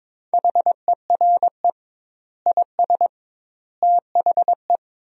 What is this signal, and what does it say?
Morse code, heard as a keyed tone, about 22 wpm, 700 Hz. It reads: HERE IS THE
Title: 2x Word-Spacing - Sentences from Top 300 Words - 22wpm